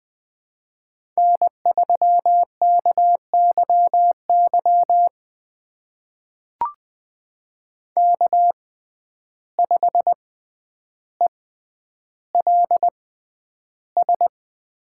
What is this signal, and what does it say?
Morse code, heard as a keyed tone, about 20 wpm, spaced as for 10 wpm, 700 Hz. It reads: N3KYY K5ELS
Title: USA Call Signs - 5 Characters - 6x Character Spacing - 20wpm